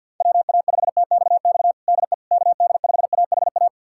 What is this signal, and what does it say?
Morse code, heard as a keyed tone, about 50 wpm, 700 Hz. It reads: WA5TXX DE KD5AHA
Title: USA Callsign DE Callsign - 6 Characters - 50wpm